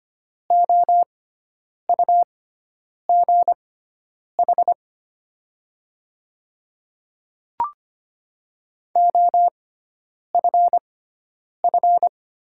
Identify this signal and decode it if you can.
Morse code, heard as a keyed tone, about 25 wpm, spaced as for 9 wpm, 700 Hz. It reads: OUGH OFF
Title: 6x Character Spacing - Top 200 Words - 25wpm